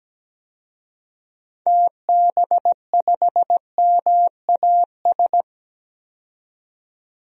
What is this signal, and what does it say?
Morse code, heard as a keyed tone, about 17 wpm, 700 Hz. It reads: TB5MAS
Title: DXCC-100 Call Signs - 6 Characters - 17wpm